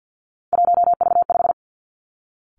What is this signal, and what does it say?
Morse code, heard as a keyed tone, about 50 wpm, 700 Hz. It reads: 1V5
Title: Three Challenging Letter-Number Combinations 50wpm